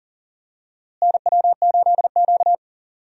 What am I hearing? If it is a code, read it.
Morse code, heard as a keyed tone, about 40 wpm, 700 Hz. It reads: NW8Q
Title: USA Call Signs - 4 Characters - Repeated 3x - 40wpm